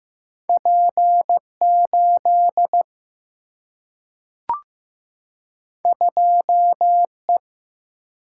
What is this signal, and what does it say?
Morse code, heard as a keyed tone, about 15 wpm, 700 Hz. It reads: P 8 2 E
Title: Single-Letter Single-Number 15wpm